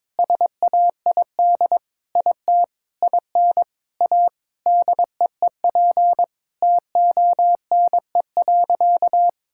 Morse code, heard as a keyed tone, about 22 wpm, 700 Hz. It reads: SAID IT IN A DEEP TONE.
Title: Sentences from Top 1600 Words 22wpm